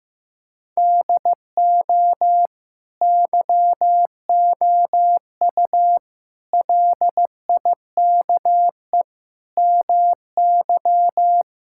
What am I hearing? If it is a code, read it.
Morse code, heard as a keyed tone, about 15 wpm, 700 Hz. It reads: DO YOU LIKE MY
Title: Sentences from Top 1000 Words - Review Part 5 - 15wpm